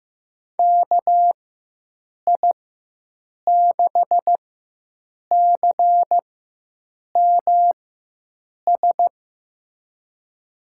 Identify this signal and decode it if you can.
Morse code, heard as a keyed tone, about 15 wpm, spaced as for 7 wpm, 700 Hz. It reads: KI6CMS
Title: USA Call Signs - 6 Characters - 4x Character Spacing - 15wpm